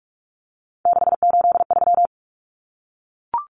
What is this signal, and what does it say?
Morse code, heard as a keyed tone, about 44 wpm, 700 Hz. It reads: 683
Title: Common Cut Numbers - 3-Digit 45wpm